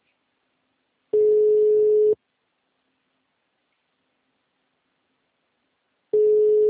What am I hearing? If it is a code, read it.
Ringback tone (Europe, 425 Hz).
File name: out- -2024- - - .42